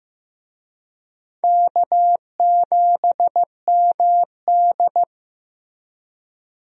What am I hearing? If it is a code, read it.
Morse code, heard as a keyed tone, about 15 wpm, 700 Hz. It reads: K7MD
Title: USA Call Signs - 4 Characters - Repeated 2x - 15wpm